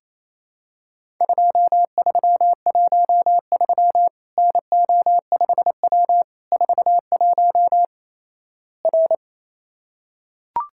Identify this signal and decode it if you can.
Morse code, heard as a keyed tone, about 28 wpm, 700 Hz. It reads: 2313 NO5W 41 R
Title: Worked All Europe DX Contest - QTC Exchange - 28wpm